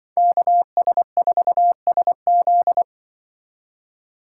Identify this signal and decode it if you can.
Morse code, heard as a keyed tone, about 25 wpm, 700 Hz. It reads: KS4SZ
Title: Speed Racing - USA 2x2 Call Signs 20wpm